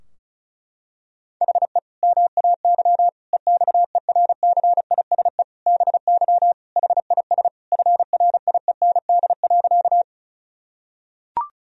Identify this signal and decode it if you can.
Morse code, heard as a keyed tone, about 36 wpm, 700 Hz. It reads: HE MAY EXERCISE BY HIS FRIEND.
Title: Sentences from Top 1000 Words - Review Part 4 - 35wpm